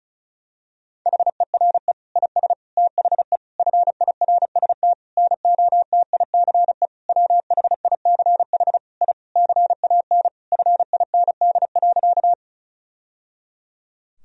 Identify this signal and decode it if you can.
Morse code, heard as a keyed tone, about 36 wpm, 700 Hz. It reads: HERE IS THE FIRST NOTICE WHICH I CAN FIND.
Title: Sentences from Top 1000 Words - Review Part 2 - 35wpm